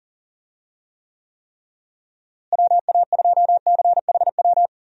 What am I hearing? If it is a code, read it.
Morse code, heard as a keyed tone, about 40 wpm, 700 Hz. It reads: WA2CHW
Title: USA Call Signs - 6 Characters - Repeated 3x - 40wpm